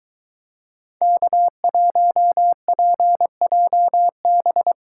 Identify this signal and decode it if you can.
Morse code, heard as a keyed tone, about 23 wpm, 700 Hz. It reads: K1PJB